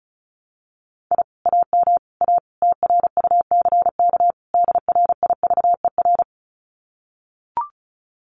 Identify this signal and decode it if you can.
Morse code, heard as a keyed tone, about 34 wpm, 700 Hz. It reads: I AM A TRUCK DRIVER
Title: Common QSO Phrases Encore 35wpm